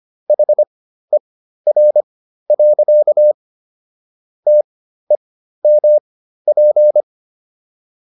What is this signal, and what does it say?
Morse code, heard as a keyed tone, about 25 wpm, spaced as for 13 wpm, 600 Hz. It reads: HER. TEMP